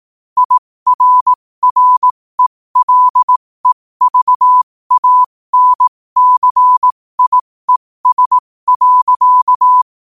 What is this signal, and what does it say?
Morse code, heard as a keyed tone, about 18 wpm, spaced as for 15 wpm, 1000 Hz. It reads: IRRELEVANCIES.
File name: qotdfiles / 15wpm